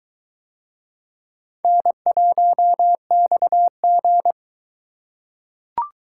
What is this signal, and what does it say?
Morse code, heard as a keyed tone, about 20 wpm, 700 Hz. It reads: N1XG